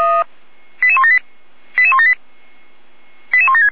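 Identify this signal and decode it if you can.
Five-tone selective call (ZVEI), heard as quick runs of five tones, tones of about 70 ms. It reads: 88178 88178 88176
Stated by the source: ZVEI five-tone selective call 88063, received on an SDR.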